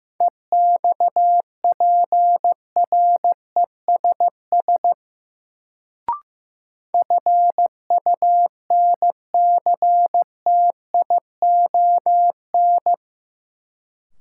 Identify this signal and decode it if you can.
Morse code, heard as a keyed tone, about 15 wpm, 700 Hz. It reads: EXPRESS FUNCTION
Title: Sentences from Top 1100 Words - New Word Drill Alt - 15wpm